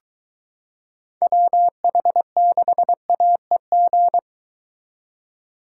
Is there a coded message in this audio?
Yes: Morse code from 1.2 to 4.2 s, heard as a keyed tone, about 22 wpm, 700 Hz. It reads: WH6AEG